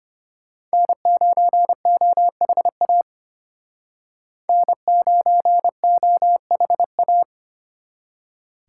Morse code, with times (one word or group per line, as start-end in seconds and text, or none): 0.7-3.0 N9OHA
4.5-7.2 N9OHA